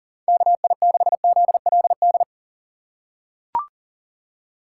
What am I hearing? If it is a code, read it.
Morse code, heard as a keyed tone, about 40 wpm, 700 Hz. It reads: KI6ZLD
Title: USA Call Signs - 6 Characters - 4x Character Spacing - 40wpm